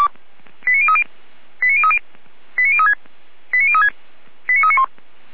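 Five-tone selective call (ZVEI), heard as quick runs of five tones, tones of about 70 ms. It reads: 89030 89030 89036 89036 89391
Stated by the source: ZVEI five-tone selective call received by SDR.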